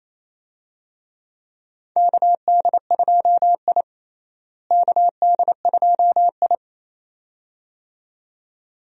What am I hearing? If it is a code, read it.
Morse code, heard as a keyed tone, about 28 wpm, 700 Hz. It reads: KD2I KD2I
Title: USA Call Signs - 4 Characters - Repeated 2x - 28wpm